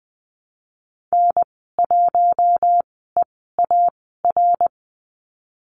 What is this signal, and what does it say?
Morse code, heard as a keyed tone, about 20 wpm, spaced as for 14 wpm, 700 Hz. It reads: N1EAR
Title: 055 Lesson 10g - Letter r - Call Signs 20wpm